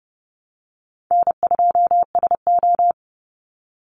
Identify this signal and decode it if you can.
Morse code, heard as a keyed tone, about 30 wpm, 700 Hz. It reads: N2SO
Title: Call Signs Repeated 2x - 30wpm